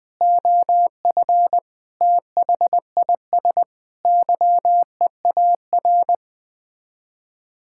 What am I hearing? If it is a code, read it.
Morse code, heard as a keyed tone, about 20 wpm, 700 Hz. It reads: OF THIS YEAR
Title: Sets of 8 Words - 20wpm